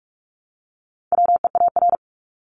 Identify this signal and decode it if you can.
Morse code, heard as a keyed tone, about 44 wpm, 700 Hz. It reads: WEAR
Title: Top 1251-1500 Words 45wpm